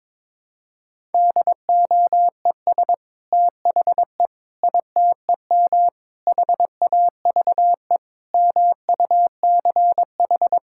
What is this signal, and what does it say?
Morse code, heard as a keyed tone, about 22 wpm, 700 Hz. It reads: DOES THE ITEM HAVE MUCH
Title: Sentences from Top 1000 Words - Review Part 6 - 22wpm